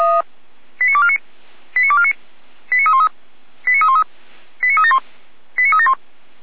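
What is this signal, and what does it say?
Five-tone selective call (ZVEI), heard as quick runs of five tones, tones of about 70 ms. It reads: 89249 89249 89313 89313 89371 89371
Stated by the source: ZVEI five-tone selective call 89276 received by SDR, logged on the feuerwehr-orlamuende.org audio server.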